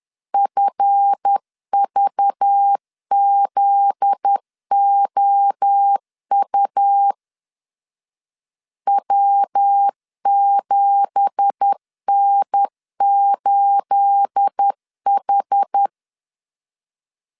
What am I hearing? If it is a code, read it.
Morse code, heard as a keyed tone, about 10 wpm, 800 Hz. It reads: FVZOU W7N8H
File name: CW / DET / 1 9 1500C S B 10wpm 4439